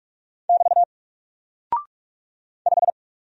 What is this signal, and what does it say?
Morse code, heard as a keyed tone, about 44 wpm, 700 Hz. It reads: <BT> 5
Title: Single Character (Letter:Number:Punctuation:Prosign) Alt 45wpm